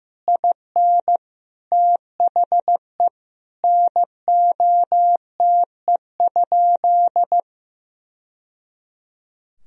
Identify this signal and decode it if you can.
Morse code, heard as a keyed tone, about 15 wpm, 700 Hz. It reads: IN THE NOTE?
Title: Sentences from Top 1000 Words - Review Part 2 - 15wpm